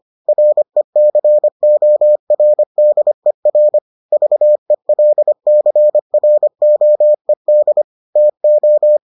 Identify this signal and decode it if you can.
Morse code, heard as a keyed tone, about 25 wpm, 600 Hz. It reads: RECORDER VELCROED TO